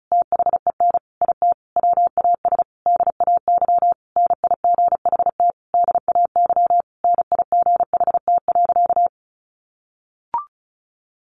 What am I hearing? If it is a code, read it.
Morse code, heard as a keyed tone, about 34 wpm, 700 Hz. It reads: THEN IT WAS DAY NIGHT DAY NIGHT.
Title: A Sound Of Thunder 35wpm